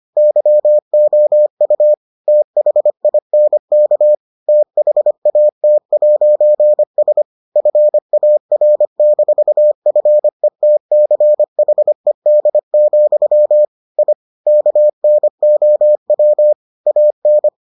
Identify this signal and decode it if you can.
Morse code, heard as a keyed tone, about 25 wpm, 600 Hz. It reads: YOU THINK THAT'S FAR-FETCHED, I KNOW AN